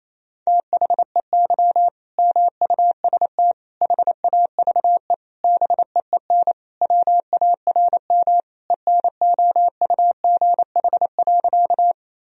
Morse code, heard as a keyed tone, about 28 wpm, 700 Hz. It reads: THEY MUST HAVE BEEN WARM ENOUGH.
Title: Sentences from Top 1000 Words - Review Part 2 - 28wpm